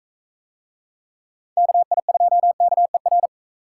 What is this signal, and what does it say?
Morse code, heard as a keyed tone, about 41 wpm, 700 Hz. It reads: KI2KER